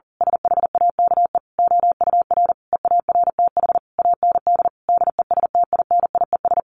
Morse code, heard as a keyed tone, about 40 wpm, 700 Hz. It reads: SHAKE OUR EARTH AND DESTINIES
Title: A Sound Of Thunder 40wpm